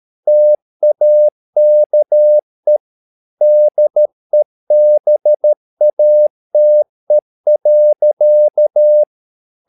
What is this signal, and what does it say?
Morse code, heard as a keyed tone, about 13 wpm, 600 Hz. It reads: TAKE DEBATE.
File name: mp3 files / Lexiphiles - 13wpm